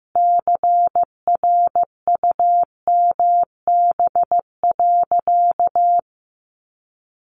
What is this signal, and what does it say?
Morse code, heard as a keyed tone, about 15 wpm, 700 Hz. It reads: CRUMB.
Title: Easy Phrases 15wpm